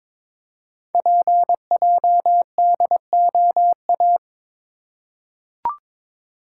Morse code, heard as a keyed tone, about 22 wpm, 700 Hz. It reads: PJDOA
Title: Code Groups - 5 Letters - 22wpm